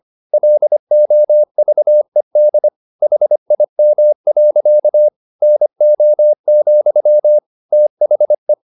Morse code, heard as a keyed tone, about 25 wpm, 600 Hz. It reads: LOVED HIM. NO, THE
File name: the old man and the sea part 1 25wpm